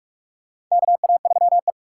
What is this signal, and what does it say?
Morse code, heard as a keyed tone, about 47 wpm, 700 Hz. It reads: KA3E
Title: USA Call Signs - 2x Character Spacing - 45wpm